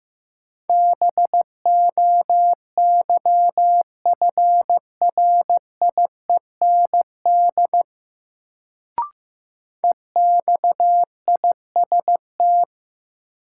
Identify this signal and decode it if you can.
Morse code, heard as a keyed tone, about 15 wpm, 700 Hz. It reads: BOYFRIEND EXIST
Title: Sentences from Top 1000 Words - Review Part 5 - New Word Drill - 15wpm